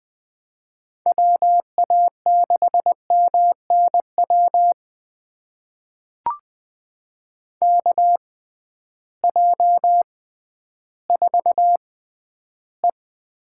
Morse code, heard as a keyed tone, about 20 wpm, spaced as for 12 wpm, 700 Hz. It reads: WA6MNW KJ4E
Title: USA Call Signs - 6 Characters - 6x Character Spacing - 20wpm